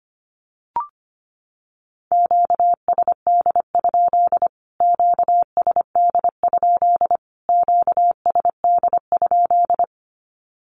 Morse code, heard as a keyed tone, about 25 wpm, 700 Hz. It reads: QSD? QSD? QSD?
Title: Q-Codes Repeated 3x 25wpm